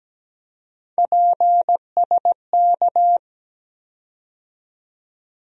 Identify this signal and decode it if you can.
Morse code, heard as a keyed tone, about 17 wpm, 700 Hz. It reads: PSK
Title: USA Call Sign Postfixes - 3 Letters 17wpm